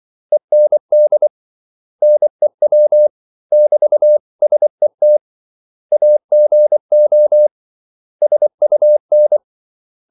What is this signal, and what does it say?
Morse code, heard as a keyed tone, about 24 wpm, 600 Hz. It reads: END NEW <BT>SET AGO SUN
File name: CW / Practice / 500MostCommonEnglishWords24